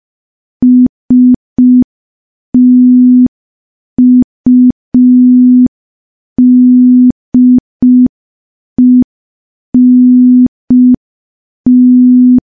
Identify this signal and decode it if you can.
Morse code, heard as a keyed tone, about 5 wpm, 250 Hz. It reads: STUDENT
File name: morse / morse-training-05WPM